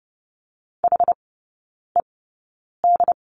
Morse code, heard as a keyed tone, about 30 wpm, spaced as for 9 wpm, 700 Hz. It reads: HED
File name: course / 30:10wpm / 087 Lesson 15e - Number 5 - Three Characters 30wpm